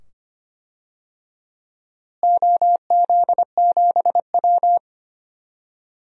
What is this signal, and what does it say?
Morse code, heard as a keyed tone, about 25 wpm, 700 Hz. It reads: OZ7W
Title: DXCC-100 Call Signs - 4 Characters - Repeated 2x - 25wpm